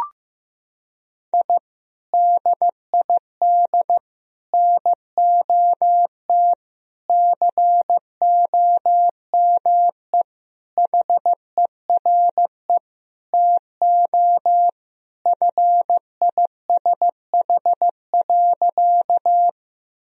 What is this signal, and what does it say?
Morse code, heard as a keyed tone, about 15 wpm, 700 Hz. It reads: I DID NOT COME HERE TO FISH.
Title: Sentences from Top 1000 Words - Review Part 1 - 15wpm